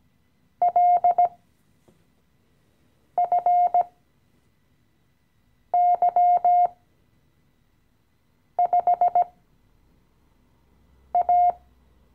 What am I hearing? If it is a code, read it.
Morse code, heard as a keyed tone, about 17 wpm, spaced as for 4 wpm, 710 Hz. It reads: LFY5A
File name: CW-Online-Kurs / Anfänger / Lektion 015